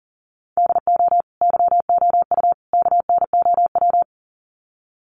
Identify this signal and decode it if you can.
Morse code, heard as a keyed tone, about 40 wpm, 700 Hz. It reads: DO YOU KNOW
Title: Sets of 3 Words 40wpm